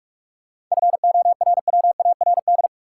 Morse code, heard as a keyed tone, about 45 wpm, 700 Hz. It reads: FORWARD
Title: Sentences from Top 1300 Words - New Word Drill - 45wpm